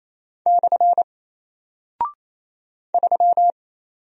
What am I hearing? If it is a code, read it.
Morse code, heard as a keyed tone, about 28 wpm, 700 Hz. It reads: / 3
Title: Single Character (Letter:Number:Punctuation:Prosign) Alt 28wpm